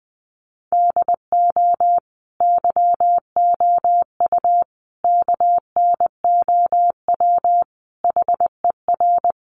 Morse code, heard as a keyed tone, about 20 wpm, 700 Hz. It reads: DO YOU KNOW HER
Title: Sentences from Top 1400 Words 20wpm